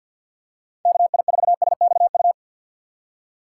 Morse code, heard as a keyed tone, about 47 wpm, 700 Hz. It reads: KI4SXU
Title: USA Call Signs - 6 Characters - 6x Character Spacing - 50wpm